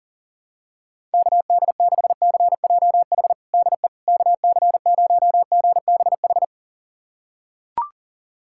Morse code, heard as a keyed tone, about 40 wpm, 700 Hz. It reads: KD6CJH DE KC0GBH
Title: USA Callsign DE Callsign 40wpm